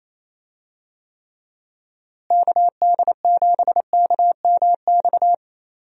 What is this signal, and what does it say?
Morse code, heard as a keyed tone, about 28 wpm, 700 Hz. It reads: KD7KMX
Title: USA Call Signs - 6 Characters - Repeated 2x - 28wpm